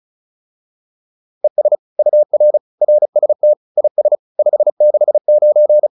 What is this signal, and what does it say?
Morse code, heard as a keyed tone, about 34 wpm, 600 Hz. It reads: ES UR RST IS 569
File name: QSO303 35